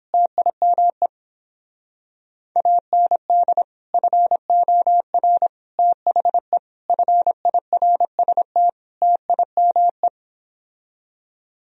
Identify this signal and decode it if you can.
Morse code, heard as a keyed tone, about 31 wpm, 700 Hz. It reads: TIME AND FOR THE FIRST TIME